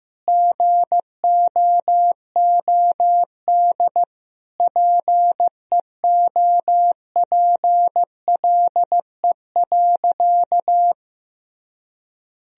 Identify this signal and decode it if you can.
Morse code, heard as a keyed tone, about 15 wpm, 700 Hz. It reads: GOOD PEOPLE.
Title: Sentences from Top 1000 Words - Review Part 3 - 15wpm